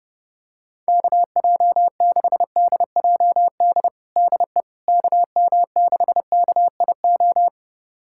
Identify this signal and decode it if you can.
Morse code, heard as a keyed tone, about 30 wpm, 700 Hz. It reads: KJ6DJD DE KM6KIO
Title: USA Callsign DE Callsign 30wpm